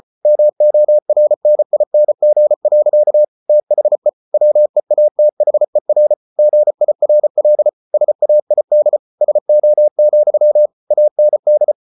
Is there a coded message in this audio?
Yes: Morse code from 0.2 to 11.7 s, heard as a keyed tone, about 34 wpm, 600 Hz. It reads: MORNING. THE WEATHER GIRL SAID SO, AND